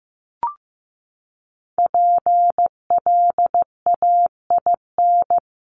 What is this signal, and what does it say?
Morse code, heard as a keyed tone, about 15 wpm, 700 Hz. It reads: PLAIN
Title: Top 500 Words 15wpm